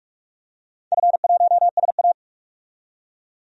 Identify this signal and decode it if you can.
Morse code, heard as a keyed tone, about 46 wpm, 700 Hz. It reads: F1SA
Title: DXCC-100 Call Signs - 4 Characters - 45wpm